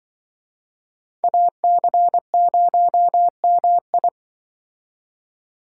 Morse code, heard as a keyed tone, about 24 wpm, 700 Hz. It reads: AC0MI
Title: Speed Racing - USA 2x2 Call Signs 20wpm